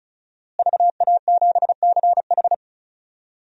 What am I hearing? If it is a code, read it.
Morse code, heard as a keyed tone, about 36 wpm, 700 Hz. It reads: VA7CH
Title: DXCC-100 Call Signs - 5 Characters - Repeated 2x - 35wpm